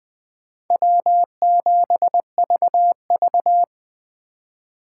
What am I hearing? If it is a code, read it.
Morse code, heard as a keyed tone, about 20 wpm, 700 Hz. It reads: W7VV